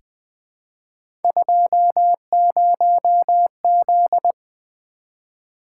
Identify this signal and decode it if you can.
Morse code, heard as a keyed tone, about 20 wpm, 700 Hz. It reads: 20Z
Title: Three Letter-Number Combinations - 20wpm